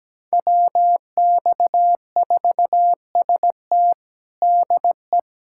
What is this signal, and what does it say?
Morse code, heard as a keyed tone, about 17 wpm, 700 Hz. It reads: WX4ST DE
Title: USA Callsign DE Callsign 17wpm